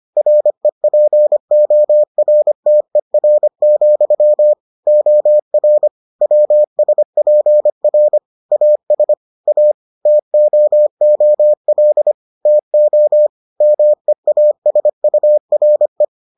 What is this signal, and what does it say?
Morse code, heard as a keyed tone, about 25 wpm, 600 Hz. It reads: REPORTER, OR WSPR AS A TOOL TO MEASURE